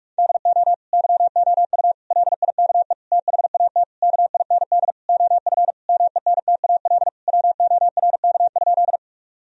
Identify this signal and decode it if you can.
Morse code, heard as a keyed tone, about 46 wpm, 700 Hz. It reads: DO YOU LIKE THAT KIND OF MENTAL WORK?
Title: Sentences from Top 1000 Words - Review Part 4 - 45wpm